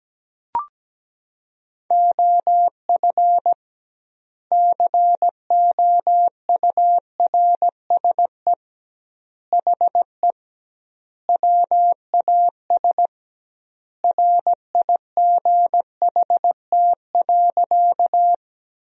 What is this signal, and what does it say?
Morse code, heard as a keyed tone, about 17 wpm, 700 Hz. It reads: OF COURSE HE WAS RIGHT.
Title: 2x Word-Spacing - Sentences from Top 500 Words - 17wpm